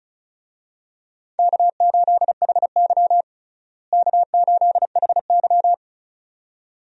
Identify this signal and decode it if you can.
Morse code, heard as a keyed tone, about 34 wpm, 700 Hz. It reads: K8HY K8HY
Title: USA Call Signs - 4 Characters - Repeated 2x - 35wpm